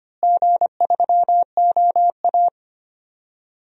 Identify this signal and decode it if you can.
Morse code, heard as a keyed tone, about 25 wpm, 700 Hz. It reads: G3OA
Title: DXCC-100 Call Signs - 4 Characters - Repeated 2x - 25wpm